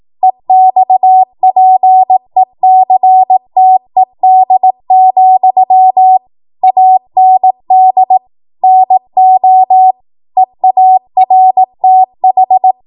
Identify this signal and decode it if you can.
Morse code, heard as a keyed tone, about 18 wpm, 750 Hz. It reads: EXPECTED, AND NO EARTH
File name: Morse / 18WPM